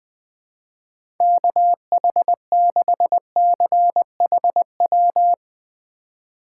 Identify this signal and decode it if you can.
Morse code, heard as a keyed tone, about 20 wpm, 700 Hz. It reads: KH6CHW
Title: USA Call Signs - 6 Characters - 2x Character Spacing - 20wpm